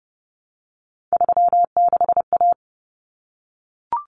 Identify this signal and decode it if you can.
Morse code, heard as a keyed tone, about 30 wpm, 700 Hz. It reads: 36A
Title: Common Cut Numbers - 3-Digit 30wpm